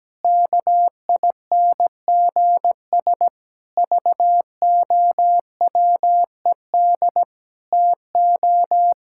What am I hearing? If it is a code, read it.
Morse code, heard as a keyed tone, about 17 wpm, 700 Hz. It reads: KINGS VOWED TO
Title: Pangram Challenge v2 - 17wpm